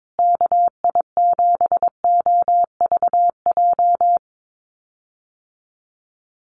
Morse code, heard as a keyed tone, about 22 wpm, 700 Hz. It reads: KI7OVJ
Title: USA Call Signs Repeated 3x 22wpm